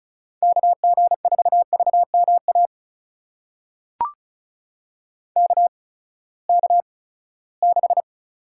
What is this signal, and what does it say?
Morse code, heard as a keyed tone, about 36 wpm, spaced as for 20 wpm, 700 Hz. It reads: KG4VMA KK6
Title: USA Call Signs - 6 Characters - 8x Character Spacing - 35wpm